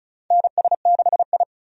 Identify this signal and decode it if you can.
Morse code, heard as a keyed tone, about 36 wpm, 700 Hz. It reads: NS6I
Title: USA Call Signs - 4 Characters - 35wpm